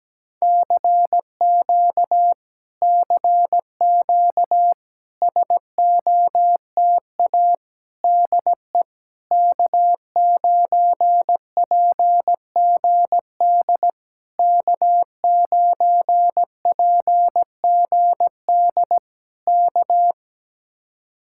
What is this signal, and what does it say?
Morse code, heard as a keyed tone, about 17 wpm, 700 Hz. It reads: CQ CQ SOTA DE K9PGD K9PGD K
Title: Speed-Racing Common QSO Phrases - 17wpm